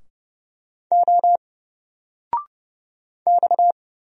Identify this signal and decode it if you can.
Morse code, heard as a keyed tone, about 30 wpm, 700 Hz. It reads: O X
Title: Single Letter-Number 30wpm